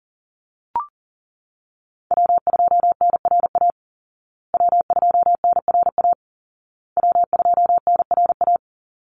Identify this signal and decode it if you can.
Morse code, heard as a keyed tone, about 40 wpm, 700 Hz. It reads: W2NRA W2NRA W2NRA
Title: Call Signs Repeated 2x - 40wpm